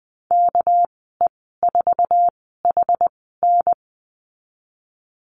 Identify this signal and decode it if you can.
Morse code, heard as a keyed tone, about 19 wpm, spaced as for 14 wpm, 700 Hz. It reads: KE4HN
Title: 245 Lesson 39g - Letter z - Call Signs 20wpm